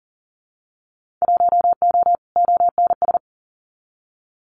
Morse code, heard as a keyed tone, about 39 wpm, 700 Hz. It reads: 1O ONS
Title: Winter Field Day Exchanges Encore 40wpm